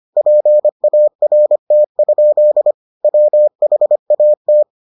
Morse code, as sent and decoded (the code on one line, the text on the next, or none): .--. .- .-. - ..--.. / .-- .... .- -
PART? WHAT